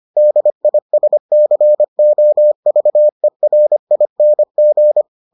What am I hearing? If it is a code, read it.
Morse code, heard as a keyed tone, about 25 wpm, 600 Hz. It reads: DISCOVERING